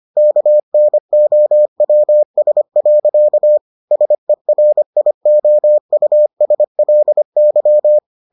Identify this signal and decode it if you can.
Morse code, heard as a keyed tone, about 25 wpm, 600 Hz. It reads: KNOWS. SERIOUSLY